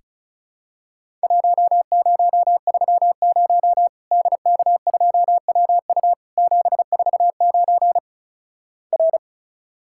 Morse code, heard as a keyed tone, about 36 wpm, 700 Hz. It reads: 1030 DK2WU 749 R
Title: Worked All Europe DX Contest - QTC Exchange - 35wpm